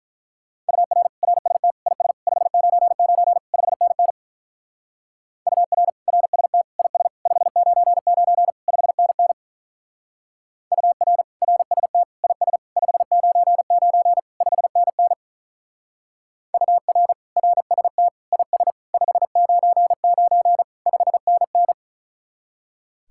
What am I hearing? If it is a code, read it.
Morse code, heard as a keyed tone, about 42 wpm, 700 Hz. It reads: UR RST IS 599 5NN UR RST IS 599 5NN UR RST IS 599 5NN UR RST IS 599 5NN
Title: Speed-Racing Common QSO Phrases - 35wpm